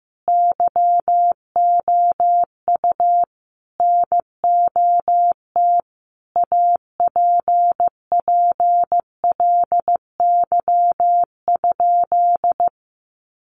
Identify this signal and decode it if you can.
Morse code, heard as a keyed tone, about 15 wpm, 700 Hz. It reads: YOU NOT APPLY?
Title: Sentences from Top 1200 Words 15wpm